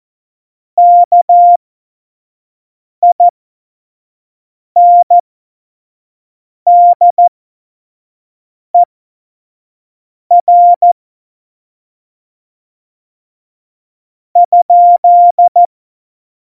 Morse code, heard as a keyed tone, about 14 wpm, spaced as for 5 wpm, 700 Hz. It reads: KINDER ?
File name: Morsekurs / Witze / Witz 18. Dezember 2018, 25 F70